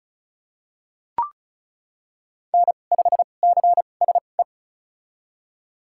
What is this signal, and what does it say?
Morse code, heard as a keyed tone, about 36 wpm, spaced as for 25 wpm, 700 Hz. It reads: N5CSE
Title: USA Call Signs - 5 Characters - 2x Character Spacing - 35wpm